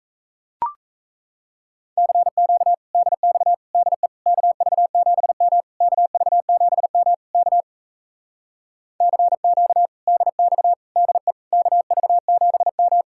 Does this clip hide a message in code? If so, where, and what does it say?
Yes: Morse code from 2.0 to 13.0 s, heard as a keyed tone, about 38 wpm, 700 Hz. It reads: CQ DX DE KV7M KV7M K CQ DX DE KV7M